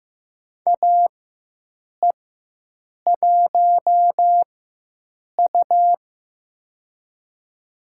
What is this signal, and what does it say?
Morse code, heard as a keyed tone, about 15 wpm, spaced as for 7 wpm, 700 Hz. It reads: AE1U
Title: USA Call Signs - 4 Characters - 4x Character Spacing - 15wpm